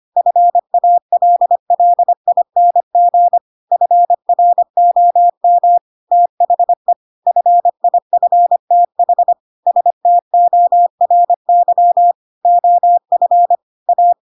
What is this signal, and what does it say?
Morse code, heard as a keyed tone, about 25 wpm, 700 Hz. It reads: FALLING FROM THE FIFTH STORY OF A
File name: Pinocchio / Pinocchio ch7 25 25 ik1hge